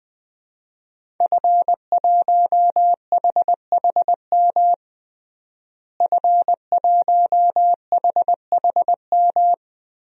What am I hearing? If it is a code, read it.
Morse code, heard as a keyed tone, about 20 wpm, 700 Hz. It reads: F1HHM F1HHM
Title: DXCC-100 Call Signs - 5 Characters - Repeated 2x - 20wpm